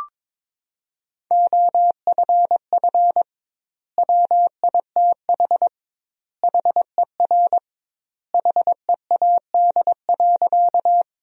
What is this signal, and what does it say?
Morse code, heard as a keyed tone, about 22 wpm, 700 Hz. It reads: OFF WITH HER HEAD.
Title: 2x Word-Spacing - Sentences from Top 300 Words - 22wpm